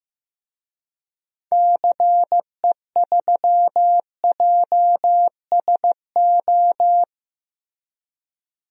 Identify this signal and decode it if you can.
Morse code, heard as a keyed tone, about 15 wpm, 700 Hz. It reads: CE3JSO
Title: DXCC-100 Call Signs - 6 Characters - Repeated 2x - 15wpm